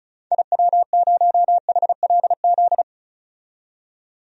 Morse code, heard as a keyed tone, about 34 wpm, 700 Hz. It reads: IW0HLZ
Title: DXCC-100 Call Signs - 6 Characters - 35wpm